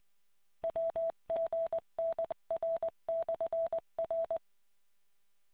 Morse code, heard as a keyed tone, about 20 wpm, 660 Hz. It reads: WGNR/R